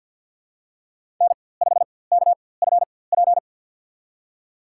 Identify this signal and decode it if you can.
Morse code, heard as a keyed tone, about 53 wpm, spaced as for 24 wpm, 700 Hz. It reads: N5KFP
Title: USA Call Signs - 5 Characters - 4x Character Spacing - 50wpm